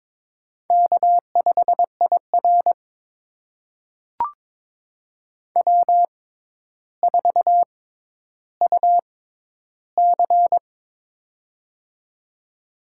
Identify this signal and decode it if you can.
Morse code, heard as a keyed tone, about 22 wpm, spaced as for 11 wpm, 700 Hz. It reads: K5IR W4UC
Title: USA Call Signs - 4 Characters - 6x Character Spacing - 22wpm